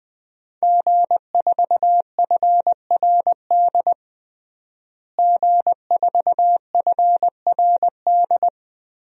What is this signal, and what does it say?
Morse code, heard as a keyed tone, about 21 wpm, 700 Hz. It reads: G4FRD G4FRD
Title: DXCC-100 Call Signs - 5 Characters - Repeated 2x - 20wpm